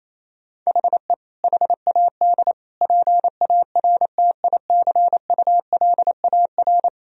Morse code, heard as a keyed tone, about 28 wpm, 700 Hz. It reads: HE HAD PARTICULAR
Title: Sentences from Top 1000 Words - Review Part 4 - 28wpm